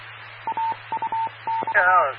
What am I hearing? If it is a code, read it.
Morse code, heard as a keyed tone, about 24 wpm, 900 Hz, with speech and radio noise. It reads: AUX